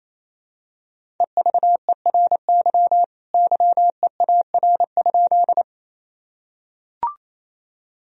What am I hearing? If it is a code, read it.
Morse code, heard as a keyed tone, about 28 wpm, 700 Hz. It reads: EVERY YEAR?
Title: Sentences from Top 1000 Words - Review Part 1 - 28wpm